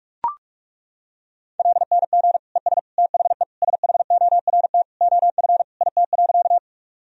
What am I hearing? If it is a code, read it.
Morse code, heard as a keyed tone, about 42 wpm, 700 Hz. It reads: LNG ES THE SHORT OF IT.
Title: Sentences from Top 400 Words and Abbreviations 45wpm